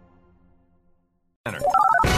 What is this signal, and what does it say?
DTMF, heard as telephone touch tones, tones of about 60 ms. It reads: 575